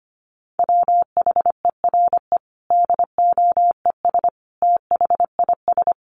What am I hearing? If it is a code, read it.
Morse code, heard as a keyed tone, about 25 wpm, 700 Hz. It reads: WHERE DOES THIS
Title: Sentences from Top 1400 Words 25wpm